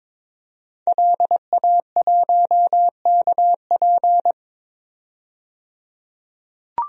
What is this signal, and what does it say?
Morse code, heard as a keyed tone, about 21 wpm, 700 Hz. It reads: LA1KP
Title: DXCC-100 Call Signs - 5 Characters - 22wpm